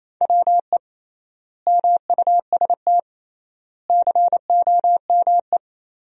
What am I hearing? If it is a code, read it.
Morse code, heard as a keyed tone, about 28 wpm, 700 Hz. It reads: WE MUST COME
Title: 3x Word-Spacing - Sentences from Top 200 Words - 28wpm